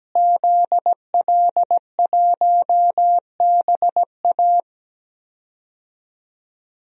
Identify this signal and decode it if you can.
Morse code, heard as a keyed tone, about 17 wpm, 700 Hz. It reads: ZL1BA
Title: DXCC-100 Call Signs - 5 Characters - Repeated 2x - 17wpm